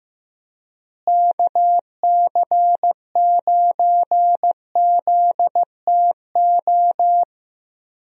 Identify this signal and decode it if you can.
Morse code, heard as a keyed tone, about 15 wpm, 700 Hz. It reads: KC9ZTO
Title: USA Call Signs - 6 Characters - Repeated 3x - 15wpm